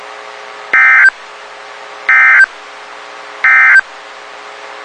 Emergency Alert System (SAME) tones, heard as harsh buzzing bursts. It is heard from 0.6 to 3.9 s.
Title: WXK91 NOAA Weather Radio